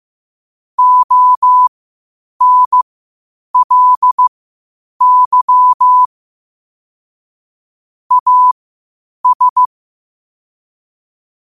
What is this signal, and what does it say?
Morse code, heard as a keyed tone, about 15 wpm, spaced as for 8 wpm, 1000 Hz. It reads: ONLY AS
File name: qotdfiles / 8wpm